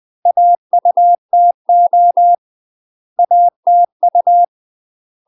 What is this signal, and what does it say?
Morse code, heard as a keyed tone, about 20 wpm, 700 Hz. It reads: AUTO ATU
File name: UK MASTER QSOs TRAINING / 20WPM / UK MASTER QSOs 0010